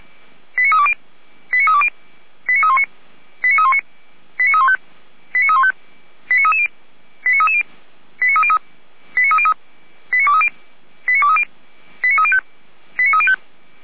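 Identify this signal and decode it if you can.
Five-tone selective call (ZVEI), heard as quick runs of five tones, tones of about 70 ms. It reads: 89320 89320 89319 89319 89315 89315 89330 89330 89303 89303 89230 89230 89305 89305
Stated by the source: ZVEI five-tone selective call 89110, received by SDR.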